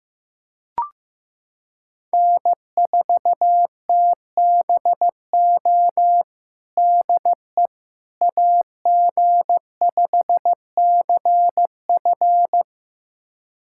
Morse code, heard as a keyed tone, about 16 wpm, 700 Hz. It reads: N4TBO DE AG5CF
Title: USA Callsign DE Callsign - 5 Characters - 15wpm